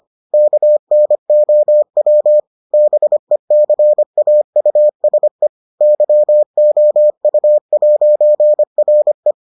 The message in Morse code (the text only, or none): KNOW BECAUSE YOU'RE